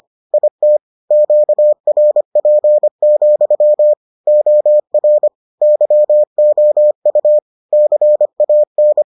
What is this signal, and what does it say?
Morse code, heard as a keyed tone, about 25 wpm, 600 Hz. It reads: IT QRP, OR YOU CAN